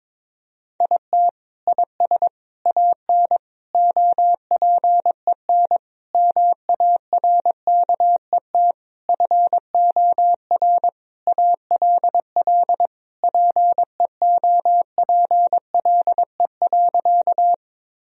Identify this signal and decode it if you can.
Morse code, heard as a keyed tone, about 22 wpm, 700 Hz. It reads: IT IS AN OPEN MARKET FOR ALL PEOPLE.
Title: Sentences from Top 1000 Words - Review Part 3 - 22wpm